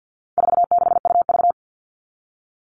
Morse code, heard as a keyed tone, about 50 wpm, 700 Hz. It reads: 46UV
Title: All Cut Numbers - 4-Digit 50wpm